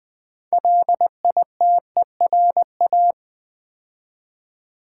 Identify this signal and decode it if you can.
Morse code, heard as a keyed tone, about 20 wpm, 700 Hz. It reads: LITERA
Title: Build A Word - 3 Letters At A Time - 8-Letter Words - 20wpm